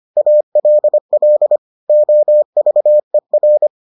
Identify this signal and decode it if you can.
Morse code, heard as a keyed tone, about 25 wpm, 600 Hz. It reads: ALL OVER